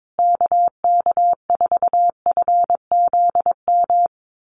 Morse code, heard as a keyed tone, about 22 wpm, 700 Hz. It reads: KK4FZM
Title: USA Call Signs Repeated 2x 22wpm